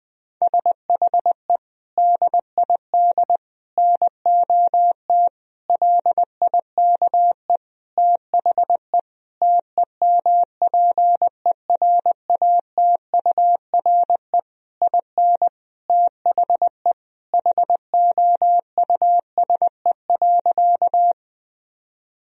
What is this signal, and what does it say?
Morse code, heard as a keyed tone, about 20 wpm, 700 Hz. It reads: SHE DID NOT LIKE THE TEMPERATURE IN THE HOUSE.
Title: Sentences from Top 1000 Words - Review Part 4 - 20wpm